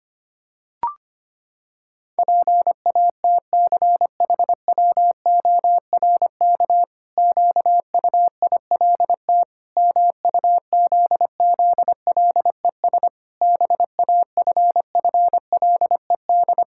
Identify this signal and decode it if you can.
Morse code, heard as a keyed tone, about 25 wpm, 700 Hz. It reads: PATCHWORK QUILT MUZZLES BAFFLED
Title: Pangram Challenge v2 - 25wpm